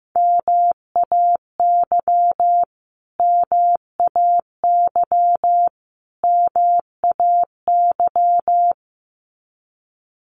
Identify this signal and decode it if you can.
Morse code, heard as a keyed tone, about 15 wpm, 700 Hz. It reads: MAY MAY MAY
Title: Top 500 Words Repeated 15wpm